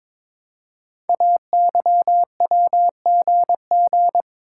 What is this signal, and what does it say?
Morse code, heard as a keyed tone, about 23 wpm, 700 Hz. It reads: AYWGG
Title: Code Groups - 5 Letters - 22wpm